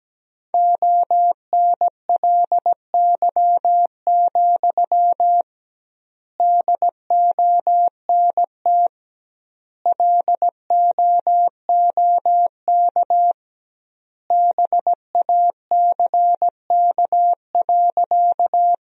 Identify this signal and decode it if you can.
Morse code, heard as a keyed tone, about 17 wpm, 700 Hz. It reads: ONLY, DONT LOOK BACK.
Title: 2x Word-Spacing - Sentences from Top 300 Words - 17wpm